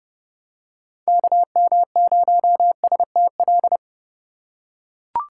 Morse code, heard as a keyed tone, about 30 wpm, 700 Hz. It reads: KM0STL
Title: USA Call Signs - 4x Character Spacing - 30wpm